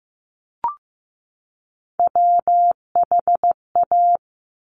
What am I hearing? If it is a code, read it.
Morse code, heard as a keyed tone, about 15 wpm, 700 Hz. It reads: WHA
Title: Top Three Letter Combinations in Words 15wpm